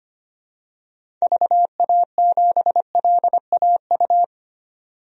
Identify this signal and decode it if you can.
Morse code, heard as a keyed tone, about 25 wpm, 700 Hz. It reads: VA7LAU
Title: DXCC-100 Call Signs - 6 Characters - 25wpm